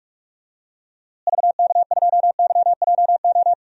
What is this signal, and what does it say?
Morse code, heard as a keyed tone, about 42 wpm, 700 Hz. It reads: VK2YJO
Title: DXCC-100 Call Signs - 6 Characters - Repeated 2x - 45wpm